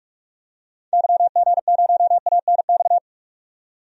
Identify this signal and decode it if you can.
Morse code, heard as a keyed tone, about 47 wpm, 700 Hz. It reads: YG0ANX
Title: DXCC-100 Call Signs - 6 Characters - 45wpm